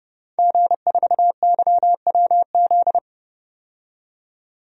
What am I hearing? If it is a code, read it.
Morse code, heard as a keyed tone, about 31 wpm, 700 Hz. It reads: G4YWZ
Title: DXCC-100 Call Signs - 5 Characters - 30wpm